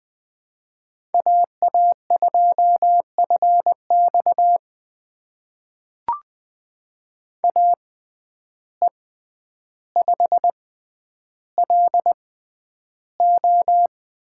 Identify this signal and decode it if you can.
Morse code, heard as a keyed tone, about 20 wpm, spaced as for 10 wpm, 700 Hz. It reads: AA2FX AE5LO
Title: USA Call Signs - 5 Characters - 6x Character Spacing - 20wpm